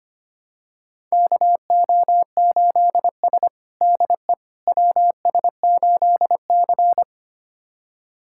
Morse code, heard as a keyed tone, about 25 wpm, 700 Hz. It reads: KO8S DE WS8C
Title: USA Callsign DE Callsign - 4 Characters - 25wpm